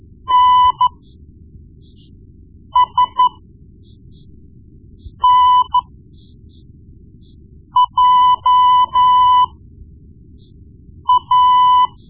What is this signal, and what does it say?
Morse code, heard as a keyed tone, about 12 wpm, spaced as for 4 wpm, 1000 Hz. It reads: NSNJA
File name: Morse Lessons / Morse Lessons / Tape-1B